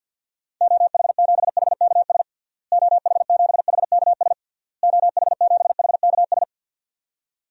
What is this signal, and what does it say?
Morse code, heard as a keyed tone, about 48 wpm, 700 Hz. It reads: OH7HKS OH7HKS OH7HKS
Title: DXCC-100 Call Signs - 6 Characters - Repeated 3x - 50wpm